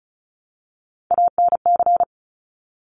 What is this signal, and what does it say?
Morse code, heard as a keyed tone, about 36 wpm, 700 Hz. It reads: ANC
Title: Top Three Letter Combinations in Words 35wpm